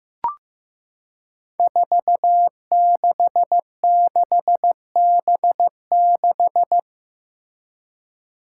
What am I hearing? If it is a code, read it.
Morse code, heard as a keyed tone, about 15 wpm, 700 Hz. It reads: 466B6
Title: Five Challenging Letter-Number Combinations 15wpm